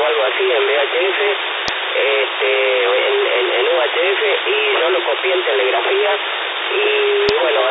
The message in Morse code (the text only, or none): none